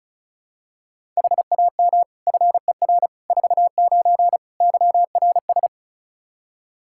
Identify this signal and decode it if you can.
Morse code, heard as a keyed tone, about 36 wpm, 700 Hz. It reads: HAM FER 49 YRS
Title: Practice QSO - Exchange 3 - More Info - 35wpm